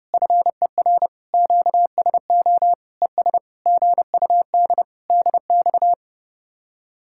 Morse code, heard as a keyed tone, about 30 wpm, 700 Hz. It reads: FER QSO ES GUD DX